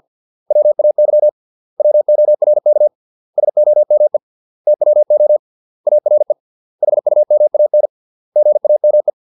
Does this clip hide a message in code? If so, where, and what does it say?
Yes: Morse code from 0.5 to 9.1 s, heard as a keyed tone, about 50 wpm, 600 Hz. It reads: WAY WORK SOME TWO ARE HUMAN GAME